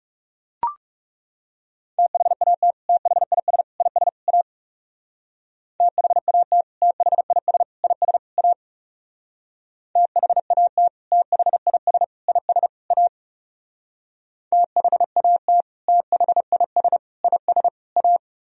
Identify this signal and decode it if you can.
Morse code, heard as a keyed tone, about 36 wpm, 700 Hz. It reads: THAT THIS IS A THAT THIS IS A THAT THIS IS A THAT THIS IS A
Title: Speed-Racing - Sets of 4 Words 30wpm